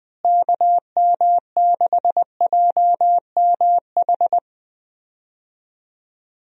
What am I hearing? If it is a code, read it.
Morse code, heard as a keyed tone, about 20 wpm, 700 Hz. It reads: KM6JMH
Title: USA Call Signs - 6 Characters - Repeated 3x - 20wpm